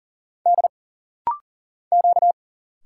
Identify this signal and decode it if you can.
Morse code, heard as a keyed tone, about 39 wpm, 700 Hz. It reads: D Q
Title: Single Letter-Number - Rapid-Fire 40wpm